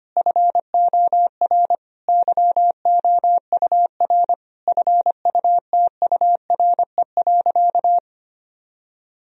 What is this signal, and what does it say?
Morse code, heard as a keyed tone, about 25 wpm, 700 Hz. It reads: FOR YOUR FUTURE.
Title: Sentences from Top 1000 Words - Review Part 4 - 25wpm